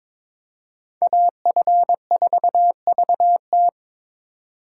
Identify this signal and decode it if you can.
Morse code, heard as a keyed tone, about 26 wpm, 700 Hz. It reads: AF4VT
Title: Speed Racing - USA 2x2 Call Signs 22wpm